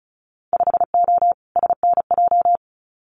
Morse code, heard as a keyed tone, about 36 wpm, 700 Hz. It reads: 5O SNJ
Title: Winter Field Day Exchanges Encore 35wpm